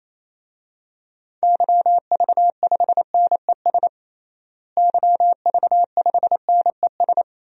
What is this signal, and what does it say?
Morse code, heard as a keyed tone, about 28 wpm, 700 Hz. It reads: YV5NES YV5NES
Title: DXCC-100 Call Signs - 6 Characters - Repeated 2x - 28wpm